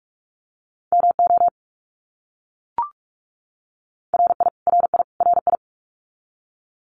Morse code, heard as a keyed tone, about 44 wpm, 700 Hz. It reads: MO RI RI RI
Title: US States Repeated 3x 45wpm